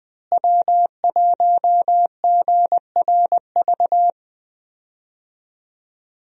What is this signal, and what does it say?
Morse code, heard as a keyed tone, about 21 wpm, 700 Hz. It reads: W1GRV
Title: USA Call Signs - 5 Characters - Repeated 2x - 20wpm